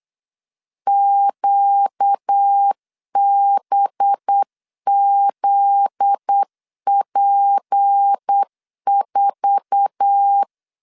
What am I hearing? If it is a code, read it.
Morse code, heard as a keyed tone, about 8 wpm, 800 Hz. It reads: QBZP4
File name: LEZIONI / 8 YJ45 / 8 YJ4 8wpm 3246